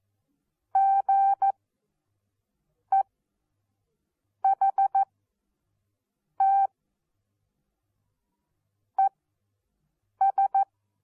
Morse code, heard as a keyed tone, about 14 wpm, spaced as for 5 wpm, 780 Hz. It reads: GEHT ES
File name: Onlinelehrgang / morsen / DJ4UF Morsen14